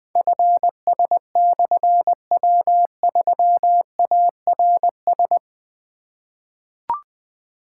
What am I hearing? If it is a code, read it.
Morse code, heard as a keyed tone, about 20 wpm, 700 Hz. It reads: FS/W3ARS
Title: Difficult Call Signs 20wpm